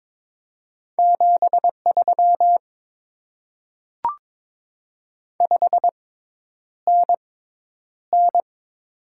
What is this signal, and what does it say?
Morse code, heard as a keyed tone, about 22 wpm, spaced as for 10 wpm, 700 Hz. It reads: 73 5NN
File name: Top 20 QSO Elements - 6x Character Spacing - 22wpm